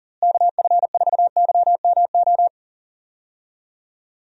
Morse code, heard as a keyed tone, about 40 wpm, 700 Hz. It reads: KF4YMO
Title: USA Call Signs - 6 Characters - Repeated 3x - 40wpm